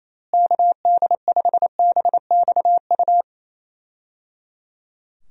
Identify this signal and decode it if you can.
Morse code, heard as a keyed tone, about 28 wpm, 700 Hz. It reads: KD5BXU
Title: USA Call Signs - 6 Characters - Repeated 3x - 28wpm